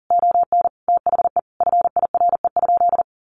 Morse code, heard as a keyed tone, about 40 wpm, 700 Hz. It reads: ON THE FIRE?
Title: Sentences from Top 1000 Words 40wpm